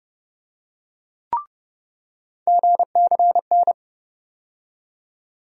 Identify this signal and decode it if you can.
Morse code, heard as a keyed tone, about 30 wpm, 700 Hz. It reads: GCN
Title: USA Call Sign Postfixes - 3 Letters 30wpm